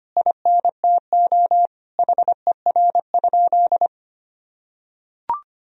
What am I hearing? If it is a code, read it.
Morse code, heard as a keyed tone, about 25 wpm, 700 Hz. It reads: INTO HER?
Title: Sentences from Top 1000 Words - Review Part 3 - 25wpm